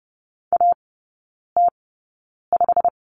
Morse code, heard as a keyed tone, about 30 wpm, spaced as for 9 wpm, 700 Hz. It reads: AT5
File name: Lesson 17e - Letter u - Three Characters 30wpm